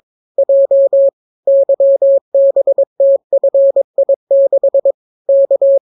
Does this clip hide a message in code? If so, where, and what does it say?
Yes: Morse code from 0.4 to 5.8 s, heard as a keyed tone, about 22 wpm, 550 Hz. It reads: J YBTFI6 K